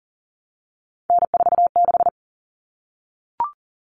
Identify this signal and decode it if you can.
Morse code, heard as a keyed tone, about 40 wpm, 700 Hz. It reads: N46
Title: Common Cut Numbers - 3-Digit 40wpm